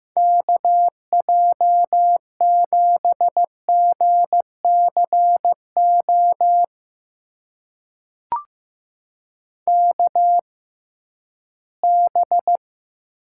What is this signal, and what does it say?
Morse code, heard as a keyed tone, about 15 wpm, spaced as for 12 wpm, 700 Hz. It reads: KJ7GCO KB
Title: USA Call Signs - 6x Character Spacing - 15wpm